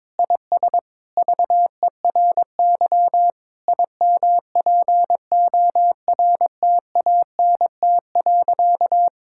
Morse code, heard as a keyed tone, about 22 wpm, 700 Hz. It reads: IS VERY IMPORTANT.
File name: practice / 22wpm / Sentences with 4-Word Phrases - 22wpm